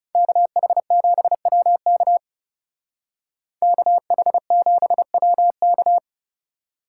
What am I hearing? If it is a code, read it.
Morse code, heard as a keyed tone, about 35 wpm, 700 Hz. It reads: KH7WK KH7WK
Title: Speed Racing - USA Call Signs 30wpm